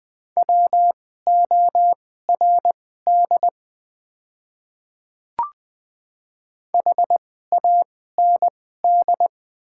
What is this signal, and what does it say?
Morse code, heard as a keyed tone, about 20 wpm, spaced as for 14 wpm, 700 Hz. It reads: WORD HAND
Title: 2x Character Spacing - Top 200 Words - 20wpm